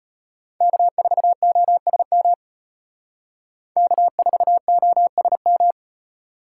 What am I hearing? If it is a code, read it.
Morse code, heard as a keyed tone, about 30 wpm, 700 Hz. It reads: K4OSM K4OSM
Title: Speed Racing - USA 1x3 Call Signs 25wpm